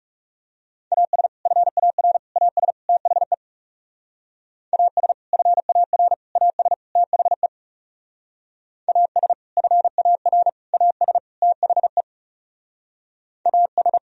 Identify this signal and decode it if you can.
Morse code, heard as a keyed tone, about 35 wpm, 700 Hz. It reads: AS FAR AS THE AS FAR AS THE AS FAR AS THE AS
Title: Speed-Racing - Sets of 4 Words 30wpm